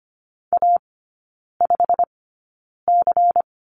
Morse code, heard as a keyed tone, about 25 wpm, spaced as for 9 wpm, 700 Hz. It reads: A5C
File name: course / 25:10wpm / 093 Lesson 16e - Letter c - Three Characters 25wpm